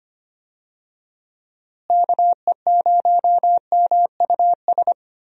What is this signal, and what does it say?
Morse code, heard as a keyed tone, about 25 wpm, 700 Hz. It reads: KE0MUS
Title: USA Call Signs - 6 Characters - Repeated 3x - 25wpm